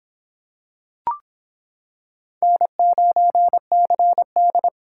Morse code, heard as a keyed tone, about 25 wpm, 700 Hz. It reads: N9CD